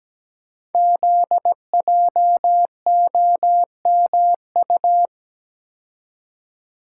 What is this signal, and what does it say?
Morse code, heard as a keyed tone, about 17 wpm, 700 Hz. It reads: ZJOMU
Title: Code Groups - 5 Letters - 17wpm